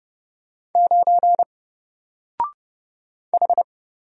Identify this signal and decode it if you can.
Morse code, heard as a keyed tone, about 29 wpm, 700 Hz. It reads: 9 H
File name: Single-Letter-Number-30wpm